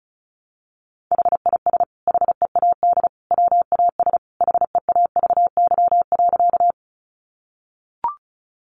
Morse code, heard as a keyed tone, about 34 wpm, 700 Hz. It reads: HIS HEAD WAS HEAVY.
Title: Sentences from Top 1100 Words 35wpm